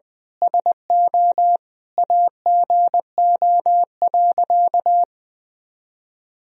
Morse code, heard as a keyed tone, about 20 wpm, 700 Hz. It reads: SO AGO.
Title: Sentences from Top 1000 Words - Review Part 3 - 20wpm